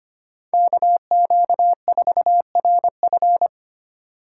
Morse code, heard as a keyed tone, about 25 wpm, 700 Hz. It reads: KQ4RF
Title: USA Call Signs - 4x Character Spacing - 25wpm